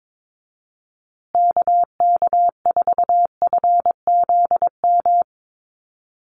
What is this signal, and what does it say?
Morse code, heard as a keyed tone, about 22 wpm, 700 Hz. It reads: KK4FZM
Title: USA Call Signs Repeated 2x 22wpm